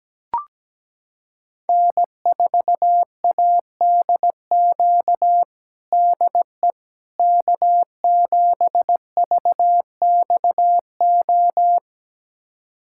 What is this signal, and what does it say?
Morse code, heard as a keyed tone, about 17 wpm, 700 Hz. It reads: N4ADQ DE K7VXO
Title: USA Callsign DE Callsign - 5 Characters - 17wpm